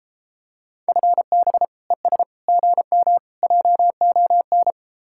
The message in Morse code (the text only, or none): FB ES GM JON